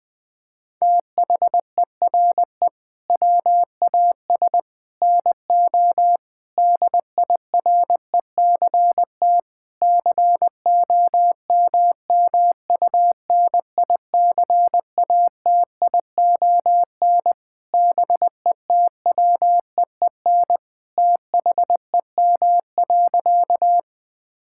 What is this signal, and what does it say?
Morse code, heard as a keyed tone, about 20 wpm, 700 Hz. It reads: THERE WAS NO DIRECT COMMUNICATION BETWEEN THEM.
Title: Sentences from Top 1000 Words - Review Part 6 - 20wpm